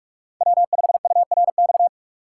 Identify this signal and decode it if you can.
Morse code, heard as a keyed tone, about 43 wpm, 700 Hz. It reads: W5URX
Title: USA Call Signs - 5 Characters - 6x Character Spacing - 45wpm